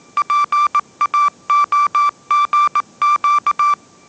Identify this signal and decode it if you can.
Morse code, heard as a keyed tone, about 19 wpm, 1170 Hz. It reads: PAOGQ
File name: mrx083 20wpm